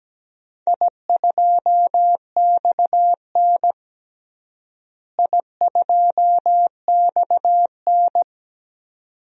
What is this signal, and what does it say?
Morse code, heard as a keyed tone, about 17 wpm, 700 Hz. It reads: I2XN I2XN
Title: DXCC-100 Call Signs - 4 Characters - Repeated 2x - 17wpm